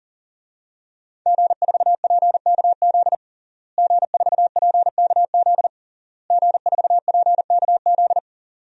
Morse code, heard as a keyed tone, about 40 wpm, 700 Hz. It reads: G4PKZ G4PKZ G4PKZ
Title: DXCC-100 Call Signs - 5 Characters - Repeated 3x - 40wpm